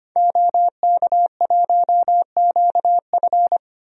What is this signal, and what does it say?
Morse code, heard as a keyed tone, about 25 wpm, 700 Hz. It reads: OK1QF
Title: DXCC-100 Call Signs - 5 Characters - Repeated 2x - 25wpm